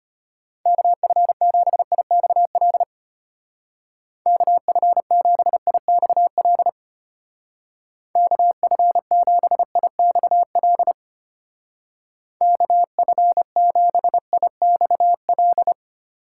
Morse code, heard as a keyed tone, about 31 wpm, 700 Hz. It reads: KF7IXL KF7IXL KF7IXL KF7IXL
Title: Speed Racing - USA 2x3 Call Signs 25wpm